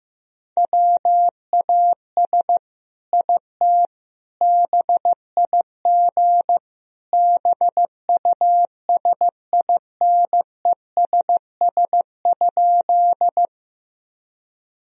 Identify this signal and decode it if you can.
Morse code, heard as a keyed tone, about 15 wpm, 700 Hz. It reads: WAS IT BIG BUSINESS?
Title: Sentences from Top 1000 Words - Review Part 1 - 15wpm